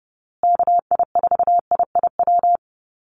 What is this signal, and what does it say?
Morse code, heard as a keyed tone, about 30 wpm, 700 Hz. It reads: KI4IIW
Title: USA Call Signs Repeated 3x 30wpm